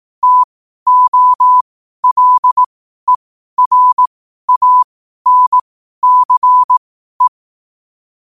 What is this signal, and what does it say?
Morse code, heard as a keyed tone, about 18 wpm, spaced as for 12 wpm, 1000 Hz. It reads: TOLERANCE
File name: qotdfiles / 12wpm